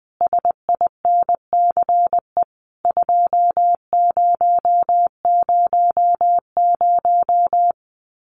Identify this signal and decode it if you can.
Morse code, heard as a keyed tone, about 20 wpm, 700 Hz. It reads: SINCE 2000
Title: Common QSO Phrases 20wpm